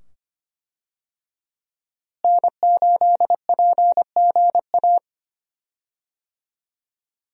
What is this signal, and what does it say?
Morse code, heard as a keyed tone, about 25 wpm, 700 Hz. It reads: N8PGA
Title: USA Call Signs - 5 Characters - Repeated 2x - 25wpm